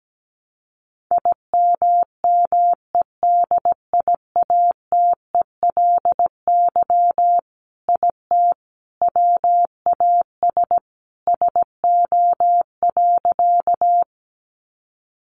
Morse code, heard as a keyed tone, about 17 wpm, 700 Hz. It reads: IMMEDIATELY IT WAS SO.
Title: Sentences from Top 1100 Words 17wpm